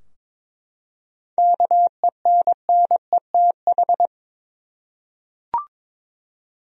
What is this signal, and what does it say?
Morse code, heard as a keyed tone, about 22 wpm, 700 Hz. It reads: KENNETH
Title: USA Common Names - 22wpm